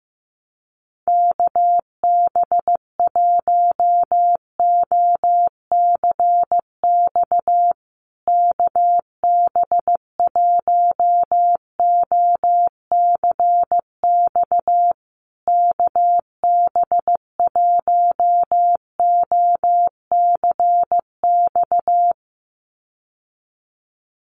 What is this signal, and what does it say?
Morse code, heard as a keyed tone, about 15 wpm, 700 Hz. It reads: KB1OCX KB1OCX KB1OCX
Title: USA Call Signs Repeated 3x 15wpm